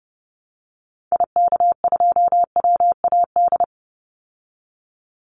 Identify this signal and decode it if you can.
Morse code, heard as a keyed tone, about 30 wpm, 700 Hz. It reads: IK2WAD
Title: Call Signs Repeated 3x - 30wpm